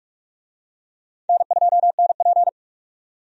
Morse code, heard as a keyed tone, about 46 wpm, 700 Hz. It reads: N2NP
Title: USA Call Signs - 4 Characters - 45wpm